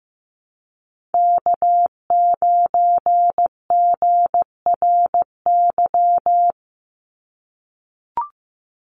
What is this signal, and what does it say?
Morse code, heard as a keyed tone, about 15 wpm, 700 Hz. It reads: K9GRY
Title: USA Call Signs 15wpm